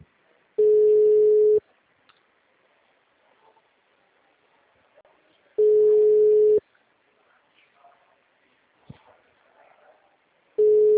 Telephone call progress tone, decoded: ringback tone

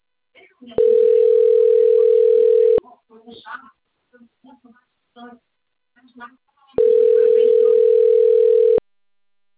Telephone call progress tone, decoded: ringback tone

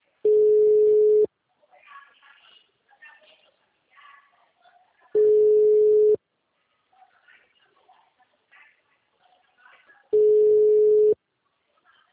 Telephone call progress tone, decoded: ringback tone

